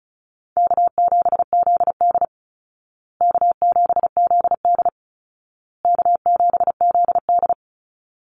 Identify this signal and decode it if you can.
Morse code, heard as a keyed tone, about 34 wpm, 700 Hz. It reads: K7ZD K7ZD K7ZD
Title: Easy Call Signs Repeated 3x - 35wpm